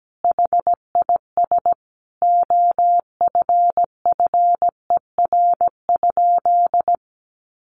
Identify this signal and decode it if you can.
Morse code, heard as a keyed tone, about 17 wpm, 700 Hz. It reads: HIS OFFER?
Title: Sentences from Top 1300 Words 17wpm